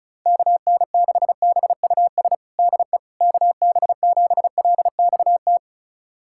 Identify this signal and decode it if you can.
Morse code, heard as a keyed tone, about 36 wpm, 700 Hz. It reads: KN6BUS DE KB7LXT
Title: USA Callsign DE Callsign - 6 Characters - 35wpm